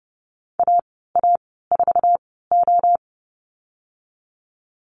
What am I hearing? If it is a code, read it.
Morse code, heard as a keyed tone, about 30 wpm, spaced as for 17 wpm, 700 Hz. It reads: AA4O
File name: Lesson 19g - Letter w - Call Signs 30wpm